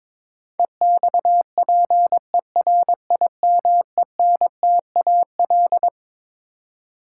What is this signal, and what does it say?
Morse code, heard as a keyed tone, about 22 wpm, 700 Hz. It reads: EXPERIMENTAL
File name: Twelve Letter Words - 22wpm